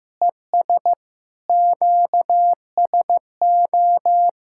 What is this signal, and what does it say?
Morse code, heard as a keyed tone, about 16 wpm, 700 Hz. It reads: ES QSO
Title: Practice QSO - Exchange 4 - Ending - 15wpm